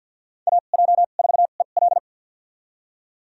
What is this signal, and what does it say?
Morse code, heard as a keyed tone, about 51 wpm, spaced as for 36 wpm, 700 Hz. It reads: AJ4EL
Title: USA Call Signs - 5 Characters - 2x Character Spacing - 50wpm